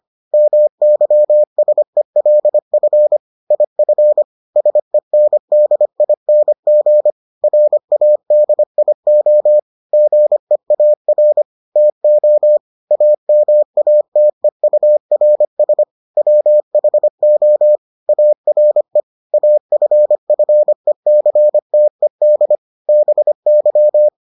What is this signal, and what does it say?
Morse code, heard as a keyed tone, about 25 wpm, 600 Hz. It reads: MYSELF IF SENDING RADIO GEAR TO AMATEURS WHO ARE AFFECTED BY